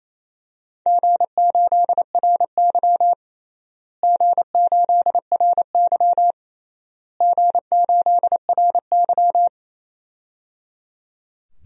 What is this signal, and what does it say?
Morse code, heard as a keyed tone, about 27 wpm, 700 Hz. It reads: G8RY G8RY G8RY
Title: DXCC-100 Call Signs - 4 Characters - Repeated 3x - 28wpm